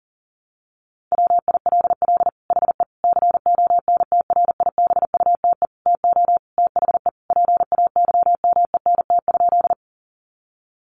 Morse code, heard as a keyed tone, about 40 wpm, 700 Hz. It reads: WILL HE CONTRIBUTE TO THE PAYMENT?
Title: Sentences from Top 1400 Words 40wpm